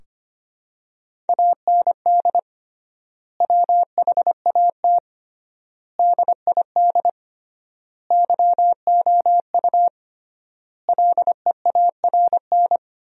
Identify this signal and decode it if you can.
Morse code, heard as a keyed tone, about 25 wpm, 700 Hz. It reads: AND WHAT DID YOU LEARN
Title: 3x Word-Spacing - Sentences from Top 300 Words - 25wpm